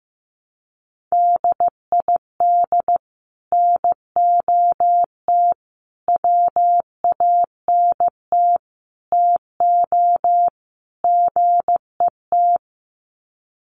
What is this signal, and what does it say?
Morse code, heard as a keyed tone, about 15 wpm, 700 Hz. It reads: DID NOT WANT TO GET
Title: Sets of 5 Words Encore2 15wpm